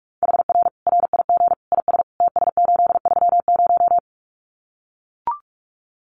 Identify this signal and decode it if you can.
Morse code, heard as a keyed tone, about 44 wpm, 700 Hz. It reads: HR RIG IS TS830
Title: Common QSO Phrases 45wpm